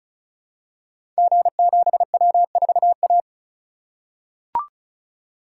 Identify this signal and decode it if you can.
Morse code, heard as a keyed tone, about 36 wpm, 700 Hz. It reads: G7W4A
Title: Code Groups - 5 Letter:Numbers - 35wpm